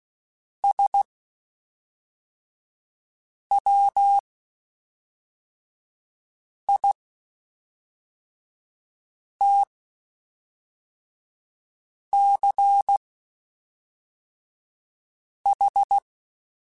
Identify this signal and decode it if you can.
Morse code, heard as a keyed tone, about 16 wpm, spaced as for 3 wpm, 800 Hz. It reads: SWITCH
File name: Spaced / Chuckles Spaced-mp3s / MiddleAge sp @5